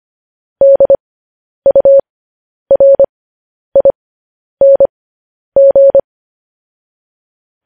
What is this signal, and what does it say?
Morse code, heard as a keyed tone, about 25 wpm, spaced as for 10 wpm, 550 Hz. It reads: DURING
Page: news-headlines-in-morse-code-at-10-wpm